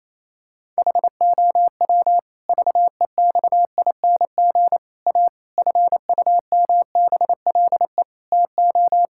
Morse code, heard as a keyed tone, about 28 wpm, 700 Hz. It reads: HOW VEXING A FUMBLE TO
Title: Pangram Challenge v2 - 28wpm